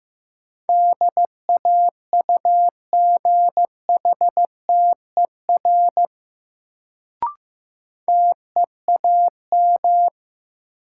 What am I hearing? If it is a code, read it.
Morse code, heard as a keyed tone, about 16 wpm, 700 Hz. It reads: DAUGHTER TEAM
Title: Sentences from Top 1500 Words - New Word Drill - 15wpm